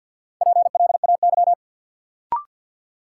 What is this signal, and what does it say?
Morse code, heard as a keyed tone, about 52 wpm, 700 Hz. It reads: PLAY
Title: Sets of 5 Words Encore - New Word Drill - 50wpm